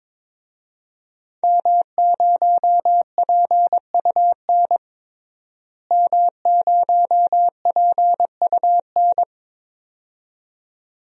Morse code, heard as a keyed tone, about 22 wpm, 700 Hz. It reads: M0PUN M0PUN
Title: DXCC-100 Call Signs - 5 Characters - Repeated 2x - 22wpm